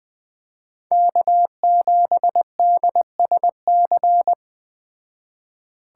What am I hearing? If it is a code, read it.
Morse code, heard as a keyed tone, about 20 wpm, 700 Hz. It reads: K7DSC
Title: Speed Racing - USA Call Signs 17wpm